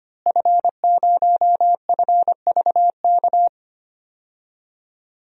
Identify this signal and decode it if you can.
Morse code, heard as a keyed tone, about 25 wpm, 700 Hz. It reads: F0FVK
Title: DXCC-100 Call Signs - 5 Characters - Repeated 3x - 25wpm